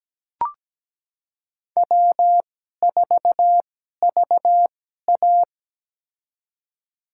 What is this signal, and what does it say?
Morse code, heard as a keyed tone, about 17 wpm, spaced as for 12 wpm, 700 Hz. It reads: W4VA
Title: USA Call Signs - 4 Characters - 2x Character Spacing - 17wpm